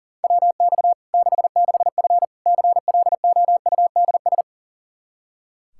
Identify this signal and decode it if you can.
Morse code, heard as a keyed tone, about 40 wpm, 700 Hz. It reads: WX 66F CLOUDS
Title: Practice QSO - Exchange 2 - Info - 40wpm